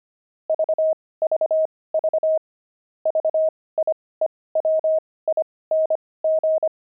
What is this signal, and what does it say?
Morse code, heard as a keyed tone, about 25 wpm, spaced as for 18 wpm, 630 Hz. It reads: VVV VIEWING
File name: pangram-fs-18-cs-25-jocks